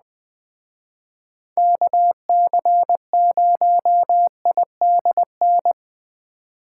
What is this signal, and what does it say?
Morse code, heard as a keyed tone, about 20 wpm, 700 Hz. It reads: KC0IDN
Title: Speed Racing - USA 2x3 Call Signs 17wpm